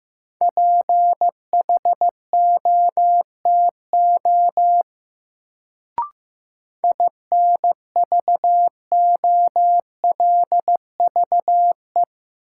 Morse code, heard as a keyed tone, about 15 wpm, 700 Hz. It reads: PHOTO INVOLVE
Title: Sentences from Top 1200 Words - New Word Drill Alt - 15wpm